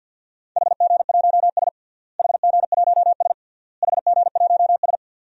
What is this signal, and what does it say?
Morse code, heard as a keyed tone, about 52 wpm, 700 Hz. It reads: HG1S HG1S HG1S
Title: DXCC-100 Call Signs - 4 Characters - Repeated 3x - 50wpm